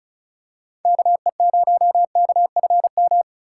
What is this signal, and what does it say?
Morse code, heard as a keyed tone, about 36 wpm, 700 Hz. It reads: KE0KFM
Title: USA Call Signs - 4x Character Spacing - 35wpm